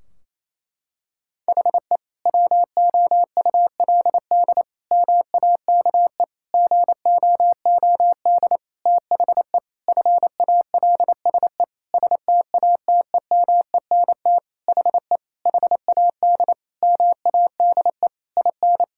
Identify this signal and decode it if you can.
Morse code, heard as a keyed tone, about 28 wpm, 700 Hz. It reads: HE WOULD MAKE GOOD THE FALSE STATEMENT HE HAD MADE IN
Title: Sentences from Top 1600 Words 28wpm